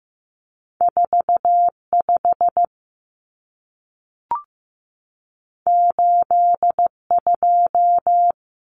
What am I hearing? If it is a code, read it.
Morse code, heard as a keyed tone, about 15 wpm, 700 Hz. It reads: 4 5 8 2
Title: Common Cut Numbers - 2-Digit 15wpm